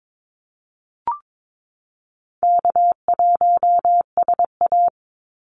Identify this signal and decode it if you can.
Morse code, heard as a keyed tone, about 22 wpm, 700 Hz. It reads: K1SA
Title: Easy Call Signs Repeated 3x - 22wpm